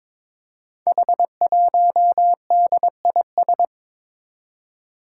Morse code, heard as a keyed tone, about 22 wpm, 700 Hz. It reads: H1DIS
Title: Five Challenging Letter-Number Combinations 22wpm